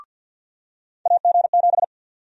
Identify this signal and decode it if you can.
Morse code, heard as a keyed tone, about 51 wpm, 700 Hz. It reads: AG7
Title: USA Call Sign Prefixes - Repeated 2x - 50wpm